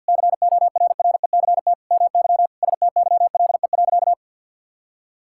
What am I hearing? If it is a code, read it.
Morse code, heard as a keyed tone, about 51 wpm, 700 Hz. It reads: CORRECT MY STYLE.
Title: Sentences from Top 1000 Words - Review Part 5 - 50wpm